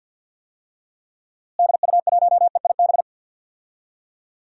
Morse code, heard as a keyed tone, about 48 wpm, 700 Hz. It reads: DU1EIB
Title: DXCC-100 Call Signs - 6 Characters - Repeated 3x - 50wpm